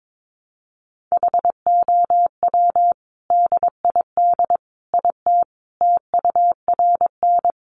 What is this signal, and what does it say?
Morse code, heard as a keyed tone, about 22 wpm, 700 Hz. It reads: HOW DID IT TURN
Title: A Sound Of Thunder 22wpm